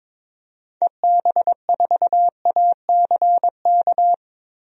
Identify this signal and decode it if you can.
Morse code, heard as a keyed tone, about 22 wpm, 700 Hz. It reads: EB4ACK
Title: DXCC-100 Call Signs - 6 Characters - 22wpm